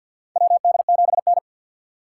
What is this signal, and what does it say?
Morse code, heard as a keyed tone, about 48 wpm, 700 Hz. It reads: WD7N
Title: USA Call Signs - 6x Character Spacing - 50wpm